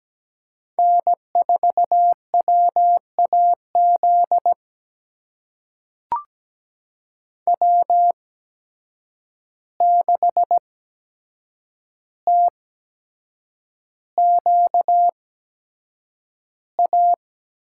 Morse code, heard as a keyed tone, about 17 wpm, spaced as for 7 wpm, 700 Hz. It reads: N4WAZ W6TQA
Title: USA Call Signs - 5 Characters - 8x Character Spacing - 17wpm